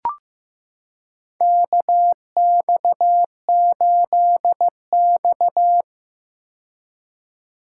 Morse code, heard as a keyed tone, about 15 wpm, 700 Hz. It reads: KX8X
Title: K1USN SST QSO Practice - 15wpm